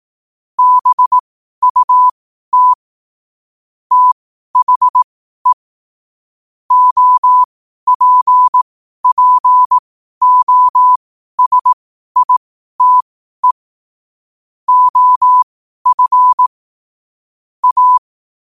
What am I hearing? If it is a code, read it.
Morse code, heard as a keyed tone, about 18 wpm, spaced as for 12 wpm, 1000 Hz. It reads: BUT THE OPPOSITE OF A
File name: qotdfiles / 12wpm